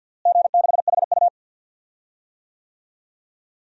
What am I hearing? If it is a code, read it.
Morse code, heard as a keyed tone, about 47 wpm, 700 Hz. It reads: G6HU
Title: DXCC-100 Call Signs - 4 Characters - Repeated 2x - 50wpm